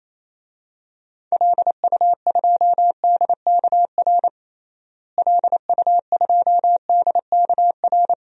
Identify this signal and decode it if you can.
Morse code, heard as a keyed tone, about 28 wpm, 700 Hz. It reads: LU2DKR LU2DKR
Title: DXCC-100 Call Signs - 6 Characters - Repeated 2x - 28wpm